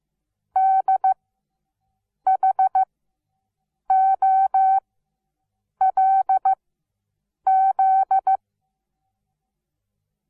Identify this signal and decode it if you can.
Morse code, heard as a keyed tone, about 15 wpm, spaced as for 7 wpm, 770 Hz. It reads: D HOLZ
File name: Onlinelehrgang / morsen / DJ4UF Morsen17